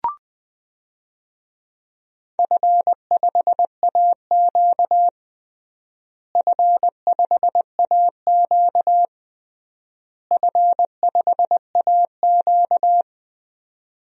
Morse code, heard as a keyed tone, about 20 wpm, 700 Hz. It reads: F5AQ F5AQ F5AQ
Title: DXCC-100 Call Signs - 4 Characters - Repeated 3x - 20wpm